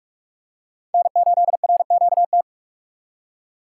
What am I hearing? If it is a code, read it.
Morse code, heard as a keyed tone, about 47 wpm, 700 Hz. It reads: N8RQT
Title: USA Call Signs - 5 Characters - 2x Character Spacing - 45wpm